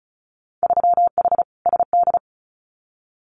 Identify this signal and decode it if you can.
Morse code, heard as a keyed tone, about 34 wpm, 700 Hz. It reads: 3H SD
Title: Winter Field Day Exchanges Encore 35wpm